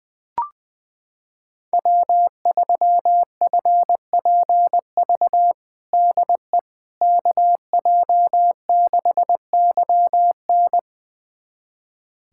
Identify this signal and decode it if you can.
Morse code, heard as a keyed tone, about 20 wpm, 700 Hz. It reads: W3FPV DE KJ6YN
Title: USA Callsign DE Callsign - 5 Characters - 20wpm